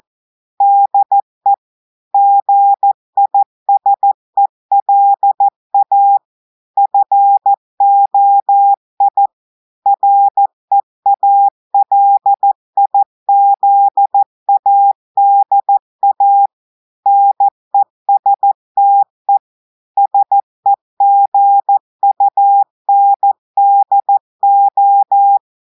Morse code, heard as a keyed tone, about 14 wpm, 800 Hz. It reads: DE GISELA FOI REALIZADA NESTE SEGUNDO
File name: audios / portugais / portugais texte en clair 1 vitesse 14 mots minutes